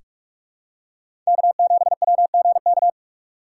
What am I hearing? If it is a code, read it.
Morse code, heard as a keyed tone, about 46 wpm, 700 Hz. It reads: K7WGK
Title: USA Call Signs - 6x Character Spacing - 45wpm